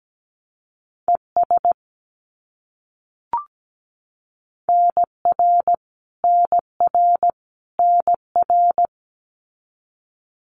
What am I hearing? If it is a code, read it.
Morse code, heard as a keyed tone, about 17 wpm, 700 Hz. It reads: ES NR NR NR
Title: Abbreviations Repeated 3x - 17wpm